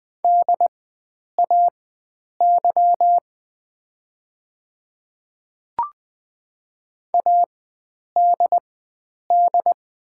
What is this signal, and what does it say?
Morse code, heard as a keyed tone, about 20 wpm, spaced as for 9 wpm, 700 Hz. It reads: DAY ADD
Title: 4x Character Spacing - Top 300 Words - 20wpm